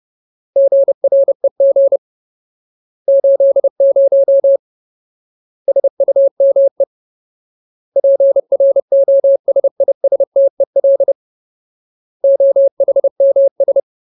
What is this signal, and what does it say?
Morse code, heard as a keyed tone, about 30 wpm, 550 Hz. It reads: GREG 80 SUME PROSISTEL OHMS
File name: Ultimate Word List / 30WPM Ultimate